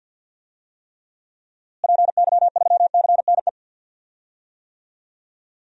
Morse code, heard as a keyed tone, about 52 wpm, 700 Hz. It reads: PY3CNE
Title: DXCC-100 Call Signs - 6 Characters - Repeated 3x - 50wpm